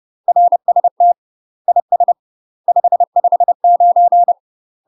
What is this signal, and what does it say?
Morse code, heard as a keyed tone, about 30 wpm, 700 Hz. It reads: RST IS 559
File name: UK MASTER QSOs TRAINING / 30WPM / UK MASTER QSOs 0030